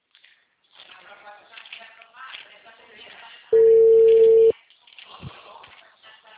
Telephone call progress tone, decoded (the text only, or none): ringback tone